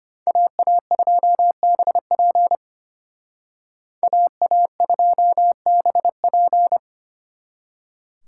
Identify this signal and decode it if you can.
Morse code, heard as a keyed tone, about 32 wpm, 700 Hz. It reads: AA2BP AA2BP